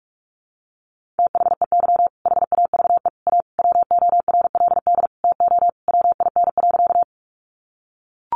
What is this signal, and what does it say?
Morse code, heard as a keyed tone, about 44 wpm, 700 Hz. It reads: THEY HAVE A WORLD TO WIN.
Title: Sentences from Top 1500 Words 45wpm